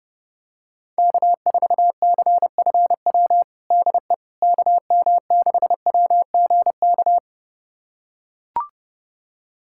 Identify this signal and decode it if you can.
Morse code, heard as a keyed tone, about 30 wpm, 700 Hz. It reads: K4CFW DE KM6WGK
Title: USA Callsign DE Callsign 30wpm